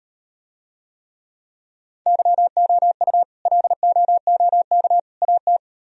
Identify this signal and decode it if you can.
Morse code, heard as a keyed tone, about 38 wpm, 700 Hz. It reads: YOU LOOK AT